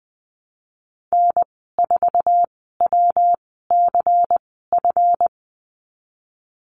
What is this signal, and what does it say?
Morse code, heard as a keyed tone, about 20 wpm, spaced as for 15 wpm, 700 Hz. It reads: N4WCF
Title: 167 Lesson 27g - Number 7 - Call Signs 20wpm